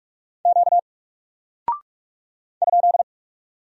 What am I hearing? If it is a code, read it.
Morse code, heard as a keyed tone, about 43 wpm, 700 Hz. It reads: Q ?
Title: Single Character (Letter:Number:Punctuation:Prosign) Alt 45wpm